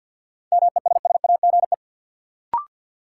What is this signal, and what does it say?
Morse code, heard as a keyed tone, about 46 wpm, 700 Hz. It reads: MESSAGE
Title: Sentences from Top 1100 Words - New Word Drill - 50wpm